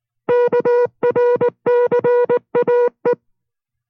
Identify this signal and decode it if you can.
Morse code, heard as a keyed tone, about 19 wpm, 480 Hz. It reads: KRCAE